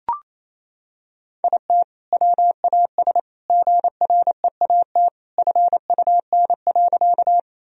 Morse code, heard as a keyed tone, about 28 wpm, 700 Hz. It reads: IT WAS GREAT FUN.
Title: Sentences from Top 1000 Words - Review Part 4 - 28wpm